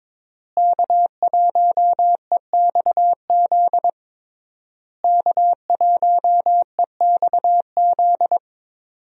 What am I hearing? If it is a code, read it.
Morse code, heard as a keyed tone, about 22 wpm, 700 Hz. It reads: K1EXZ K1EXZ
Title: USA Call Signs - 5 Characters - Repeated 3x - 22wpm